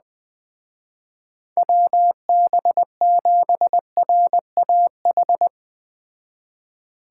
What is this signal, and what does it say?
Morse code, heard as a keyed tone, about 19 wpm, 700 Hz. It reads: WB7RAH
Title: Speed Racing - USA 2x3 Call Signs 17wpm